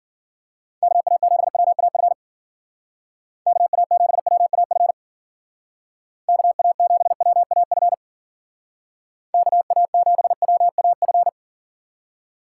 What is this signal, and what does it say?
Morse code, heard as a keyed tone, about 49 wpm, 700 Hz. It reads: KA7WAF KA7WAF KA7WAF KA7WAF
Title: Speed Racing - USA 2x3 Call Signs 40wpm